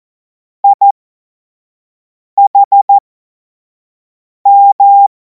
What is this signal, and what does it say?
Morse code, heard as a keyed tone, about 14 wpm, spaced as for 5 wpm, 800 Hz. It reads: IHM
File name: Morsekurs / Witze / Witz 07. Juli 2026 25 F70